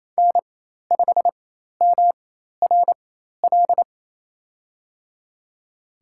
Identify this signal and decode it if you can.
Morse code, heard as a keyed tone, about 27 wpm, spaced as for 13 wpm, 700 Hz. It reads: N5MRL
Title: USA Call Signs - 5 Characters - 4x Character Spacing - 28wpm